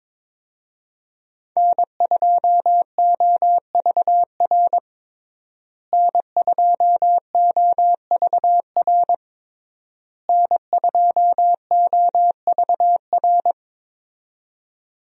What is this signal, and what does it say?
Morse code, heard as a keyed tone, about 22 wpm, 700 Hz. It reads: N2OVR N2OVR N2OVR
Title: USA Call Signs - 5 Characters - Repeated 3x - 22wpm